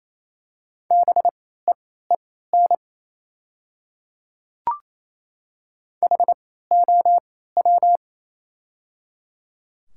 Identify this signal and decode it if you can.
Morse code, heard as a keyed tone, about 28 wpm, spaced as for 16 wpm, 700 Hz. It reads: BEEN HOW
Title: 3x Character Spacing - Top 300 Words - 28wpm